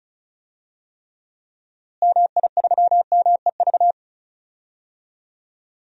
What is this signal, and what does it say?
Morse code, heard as a keyed tone, about 36 wpm, 700 Hz. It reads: MI3MEV
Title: DXCC-100 Call Signs - 6 Characters - Repeated 2x - 35wpm